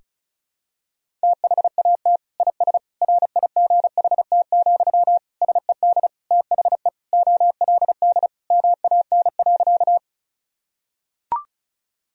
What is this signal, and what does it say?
Morse code, heard as a keyed tone, about 36 wpm, 700 Hz. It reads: THAT IS RIGHT, SED THE OLD MAN.
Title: Sentences from Top 200 Words and Abbreviations 35wpm